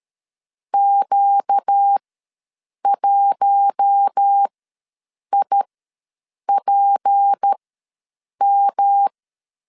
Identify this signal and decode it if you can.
Morse code, heard as a keyed tone, about 13 wpm, spaced as for 7 wpm, 800 Hz. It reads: Q1IPM
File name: LEZIONI / 6 PKX0 / 6 P ws 12wpm 3117